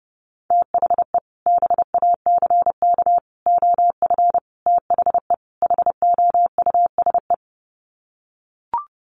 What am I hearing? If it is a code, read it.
Morse code, heard as a keyed tone, about 30 wpm, 700 Hz. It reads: THE BACK OF THE HOUSE
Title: Sets of 5 Words Encore2 30wpm